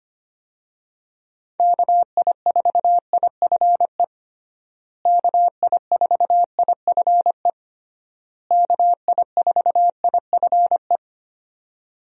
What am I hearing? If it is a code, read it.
Morse code, heard as a keyed tone, about 25 wpm, 700 Hz. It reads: KI4IFE KI4IFE KI4IFE
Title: USA Call Signs - 6 Characters - Repeated 3x - 25wpm